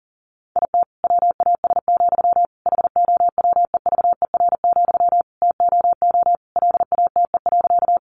Morse code, heard as a keyed tone, about 40 wpm, 700 Hz. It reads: IT WAS, HOWEVER, TOO LATE.
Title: Sentences from Top 600 Words 40wpm